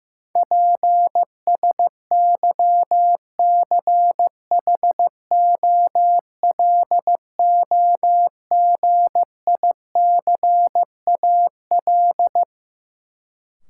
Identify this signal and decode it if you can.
Morse code, heard as a keyed tone, about 15 wpm, 700 Hz. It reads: PSYCHOLOGICAL
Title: Sentences from Top 1400 Words - New Word Drill Alt - 15wpm